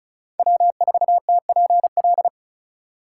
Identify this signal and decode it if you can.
Morse code, heard as a keyed tone, about 34 wpm, 700 Hz. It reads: W4TPL
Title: USA Call Signs - 5 Characters - 2x Character Spacing - 35wpm